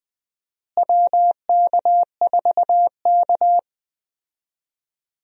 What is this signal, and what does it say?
Morse code, heard as a keyed tone, about 20 wpm, 700 Hz. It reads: WK4K
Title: USA Call Signs - 4 Characters - Repeated 3x - 20wpm